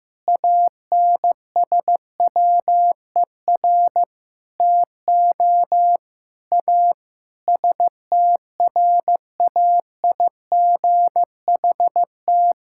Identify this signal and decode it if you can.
Morse code, heard as a keyed tone, about 15 wpm, 700 Hz. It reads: ANSWER TO A STRAIGHT
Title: Sentences from Top 1000 Words - Review Part 6 - 15wpm